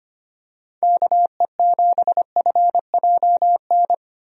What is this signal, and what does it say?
Morse code, heard as a keyed tone, about 25 wpm, 700 Hz. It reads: KE7FJN
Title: USA Call Signs - 6 Characters - 25wpm